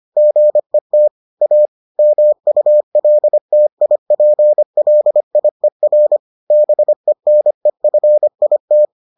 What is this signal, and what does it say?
Morse code, heard as a keyed tone, about 25 wpm, 600 Hz. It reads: GET A MULTIPLIER BENEFIT